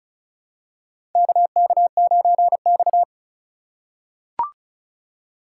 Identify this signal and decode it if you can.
Morse code, heard as a keyed tone, about 36 wpm, 700 Hz. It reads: KK9X
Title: USA Call Signs - 4 Characters - 3x Character Spacing - 35wpm